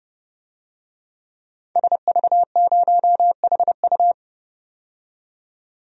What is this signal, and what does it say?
Morse code, heard as a keyed tone, about 31 wpm, 700 Hz. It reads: SV0HU
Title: DXCC-100 Call Signs - 5 Characters - 30wpm